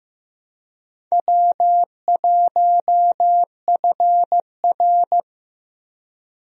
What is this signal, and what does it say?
Morse code, heard as a keyed tone, about 15 wpm, 700 Hz. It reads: W1FR
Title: USA Call Signs - 4 Characters - Repeated 3x - 15wpm